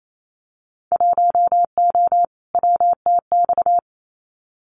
Morse code, heard as a keyed tone, about 28 wpm, 700 Hz. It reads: 1O WTX
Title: Winter Field Day Exchanges Encore 28wpm